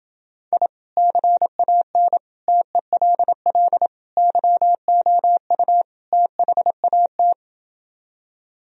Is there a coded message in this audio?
Yes: Morse code from 0.5 to 7.3 s, heard as a keyed tone, about 28 wpm, 700 Hz. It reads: I CAN TELL YOU THAT